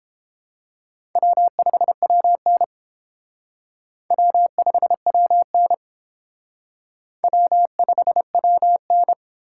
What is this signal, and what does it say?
Morse code, heard as a keyed tone, about 31 wpm, 700 Hz. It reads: W5WN W5WN W5WN